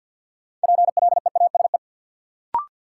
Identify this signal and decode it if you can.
Morse code, heard as a keyed tone, about 47 wpm, 700 Hz. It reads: PLEASE
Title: Sentences from Top 1500 Words - New Word Drill - 50wpm